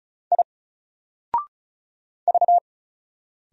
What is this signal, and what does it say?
Morse code, heard as a keyed tone, about 35 wpm, 700 Hz. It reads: I V
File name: Single-Letter-Number-35wpm